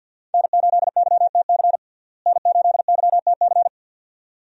Morse code, heard as a keyed tone, about 51 wpm, 700 Hz. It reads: N8YTC N8YTC
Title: USA Call Signs - 5 Characters - Repeated 2x - 50wpm